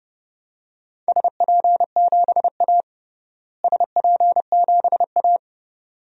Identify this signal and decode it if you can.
Morse code, heard as a keyed tone, about 30 wpm, 700 Hz. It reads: SP7A SP7A
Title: DXCC-100 Call Signs - 4 Characters - Repeated 2x - 30wpm